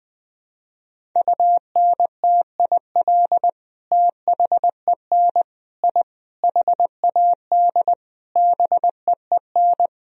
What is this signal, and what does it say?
Morse code, heard as a keyed tone, about 20 wpm, 700 Hz. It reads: UNTIL THEN I HAD BEEN
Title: Sentences from Top 1000 Words - Review Part 1 - 20wpm